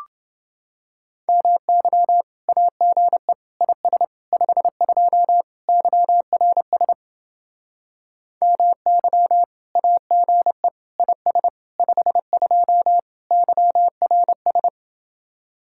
Morse code, heard as a keyed tone, about 29 wpm, 700 Hz. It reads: MY AGE IS 52 YRS MY AGE IS 52 YRS